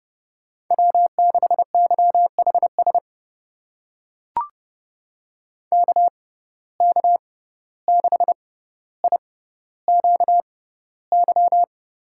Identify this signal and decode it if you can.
Morse code, heard as a keyed tone, about 30 wpm, spaced as for 14 wpm, 700 Hz. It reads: W6YHS KK6IQY
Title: USA Call Signs - 6x Character Spacing - 30wpm